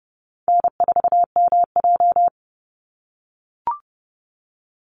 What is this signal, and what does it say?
Morse code, heard as a keyed tone, about 30 wpm, 700 Hz. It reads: N4MJ
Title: NA SOTA Chaser Callsigns 30wpm